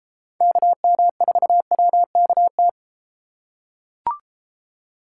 Morse code, heard as a keyed tone, about 33 wpm, 700 Hz. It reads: KM4WKT